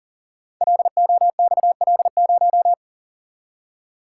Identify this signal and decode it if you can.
Morse code, heard as a keyed tone, about 41 wpm, 700 Hz. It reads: LOXL0
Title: Code Groups - 5 Letter:Numbers - 40wpm